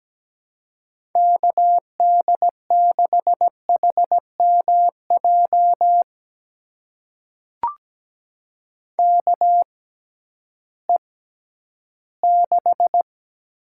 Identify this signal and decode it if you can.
Morse code, heard as a keyed tone, about 17 wpm, spaced as for 11 wpm, 700 Hz. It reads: KD6HMJ KE6
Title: USA Call Signs - 6 Characters - 6x Character Spacing - 17wpm